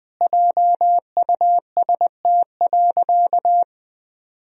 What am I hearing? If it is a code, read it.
Morse code, heard as a keyed tone, about 20 wpm, 700 Hz. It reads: JUST.
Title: Sentences from Top 1000 Words - Review Part 1 - 20wpm